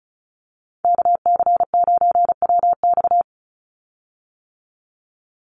Morse code, heard as a keyed tone, about 34 wpm, 700 Hz. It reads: KC9WX
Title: USA Call Signs 35wpm